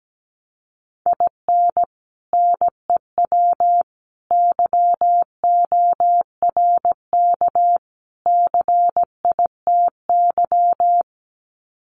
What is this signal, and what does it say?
Morse code, heard as a keyed tone, about 17 wpm, 700 Hz. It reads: IN NEW YORK CITY
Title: Sets of 4 Words 17wpm